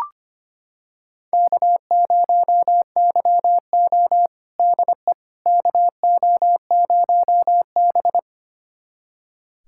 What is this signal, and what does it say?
Morse code, heard as a keyed tone, about 25 wpm, 700 Hz. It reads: K0YO DE KO0B
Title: USA Callsign DE Callsign - 4 Characters - 25wpm